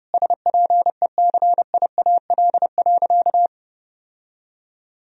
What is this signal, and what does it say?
Morse code, heard as a keyed tone, about 30 wpm, 700 Hz. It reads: SPECIAL.
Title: Sentences from Top 1000 Words - Review Part 3 - 30wpm